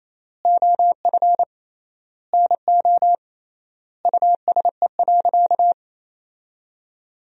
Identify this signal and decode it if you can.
Morse code, heard as a keyed tone, about 28 wpm, 700 Hz. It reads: OF NO USE.
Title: 3x Word-Spacing - Sentences from Top 100 Words - 28wpm